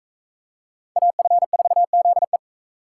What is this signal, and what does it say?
Morse code, heard as a keyed tone, about 41 wpm, 700 Hz. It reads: AF4ZE